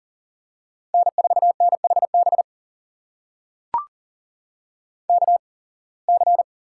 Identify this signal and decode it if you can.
Morse code, heard as a keyed tone, about 40 wpm, spaced as for 26 wpm, 700 Hz. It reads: N4NHB KC
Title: USA Call Signs - 8x Character Spacing - 40wpm